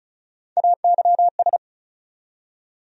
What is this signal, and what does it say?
Morse code, heard as a keyed tone, about 34 wpm, 700 Hz. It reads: AYS
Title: USA Call Sign Postfixes - 3 Letters 35wpm